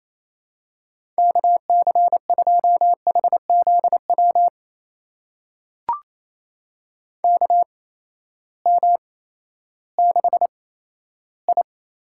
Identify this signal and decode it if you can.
Morse code, heard as a keyed tone, about 28 wpm, 700 Hz. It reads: K C 2 H Z W K M 6 I
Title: USA Call Signs - 6 Characters - 8x Character Spacing - 28wpm